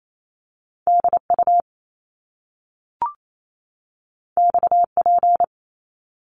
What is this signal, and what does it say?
Morse code, heard as a keyed tone, about 28 wpm, 700 Hz. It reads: DU XP
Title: Top Two Letter Combinations in Words 28wpm